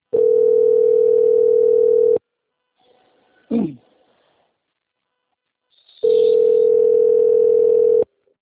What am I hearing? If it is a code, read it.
Ringback tone (North America), 440 + 480 Hz.